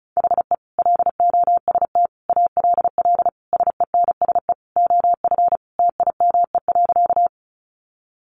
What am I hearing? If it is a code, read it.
Morse code, heard as a keyed tone, about 34 wpm, 700 Hz. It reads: HE LOST ALL SENSE OF TIME.
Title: Sentences from Top 1400 Words 35wpm